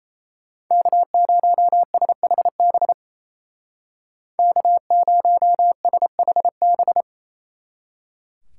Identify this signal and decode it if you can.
Morse code, heard as a keyed tone, about 34 wpm, 700 Hz. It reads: K0SHB K0SHB
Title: Speed Racing - USA 1x3 Call Signs 28wpm